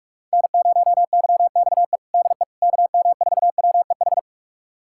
Morse code, heard as a keyed tone, about 46 wpm, 700 Hz. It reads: N0YXE DE KM4WEH
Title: USA Callsign DE Callsign 45wpm